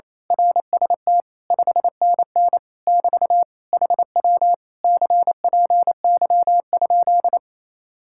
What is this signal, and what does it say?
Morse code, heard as a keyed tone, about 26 wpm, 700 Hz. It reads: RST 5NN <BT> HW CPY?
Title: Speed-Racing Common QSO Phrases - 28wpm